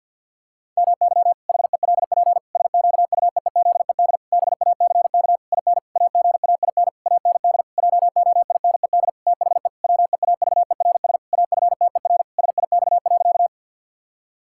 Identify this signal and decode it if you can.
Morse code, heard as a keyed tone, about 50 wpm, 700 Hz. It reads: MY HELP SQUEEZED BACK IN AGAIN AND JOINED THE WEAVERS AFTER SIX.
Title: Pangram Challenge v2 - 50wpm